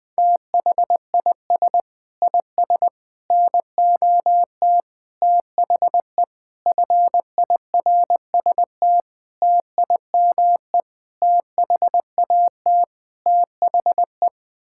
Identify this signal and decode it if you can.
Morse code, heard as a keyed tone, about 20 wpm, 700 Hz. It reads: THIS IS NOT THE FIRST TIME THAT THE
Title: Sets of 8 Words - 20wpm